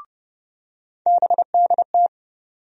Morse code, heard as a keyed tone, about 30 wpm, 700 Hz. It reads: BDT
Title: USA Call Sign Postfixes - 3 Letters 30wpm